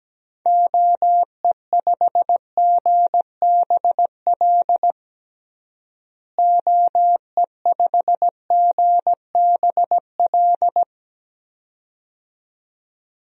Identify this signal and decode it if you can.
Morse code, heard as a keyed tone, about 17 wpm, 700 Hz. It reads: OE5GBL OE5GBL
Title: DXCC-100 Call Signs - 6 Characters - Repeated 2x - 17wpm